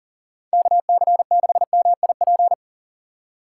Morse code, heard as a keyed tone, about 51 wpm, 700 Hz. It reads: KC6MIP